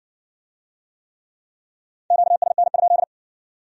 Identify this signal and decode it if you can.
Morse code, heard as a keyed tone, about 58 wpm, 700 Hz. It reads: QSN?